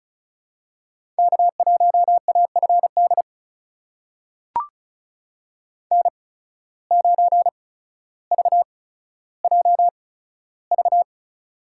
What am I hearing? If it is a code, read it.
Morse code, heard as a keyed tone, about 34 wpm, spaced as for 15 wpm, 700 Hz. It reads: K1AFD N9VJV
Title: USA Call Signs - 5 Characters - 8x Character Spacing - 35wpm